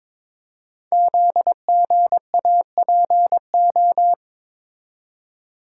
Morse code, heard as a keyed tone, about 22 wpm, 700 Hz. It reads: ZGAPO
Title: Code Groups - 5 Letters - 22wpm